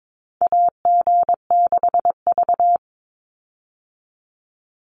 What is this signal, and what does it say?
Morse code, heard as a keyed tone, about 22 wpm, 700 Hz. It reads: AG6V
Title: Call Signs - 22wpm